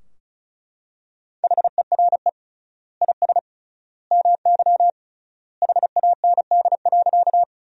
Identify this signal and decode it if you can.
Morse code, heard as a keyed tone, about 36 wpm, 700 Hz. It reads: HERE IS MY HAND.
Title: 3x Word-Spacing - Sentences from Top 200 Words - 35wpm